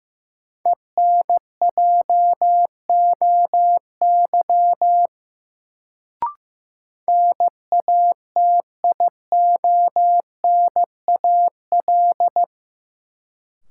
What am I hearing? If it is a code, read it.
Morse code, heard as a keyed tone, about 16 wpm, 700 Hz. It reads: ENJOY NATIONAL
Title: Sentences from Top 1000 Words - Review Part 4 - New Word Drill - 15wpm